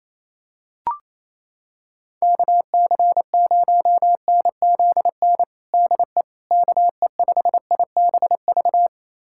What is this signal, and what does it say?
Morse code, heard as a keyed tone, about 28 wpm, 700 Hz. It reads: KC0NZN DE KE5IBV
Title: USA Callsign DE Callsign 28wpm